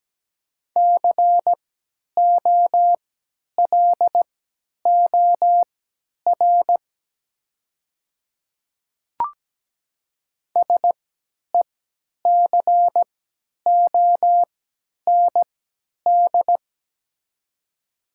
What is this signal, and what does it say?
Morse code, heard as a keyed tone, about 17 wpm, spaced as for 10 wpm, 700 Hz. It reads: COLOR SECOND
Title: 3x Character Spacing - Top 300 Words - 17wpm